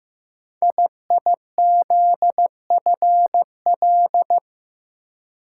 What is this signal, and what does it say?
Morse code, heard as a keyed tone, about 15 wpm, 700 Hz. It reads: IIZFL
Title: Code Groups - 5 Letters - 15wpm